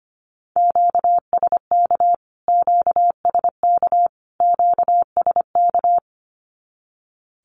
Morse code, heard as a keyed tone, about 25 wpm, 700 Hz. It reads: QSK QSK QSK
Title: Q-Codes Repeated 3x 25wpm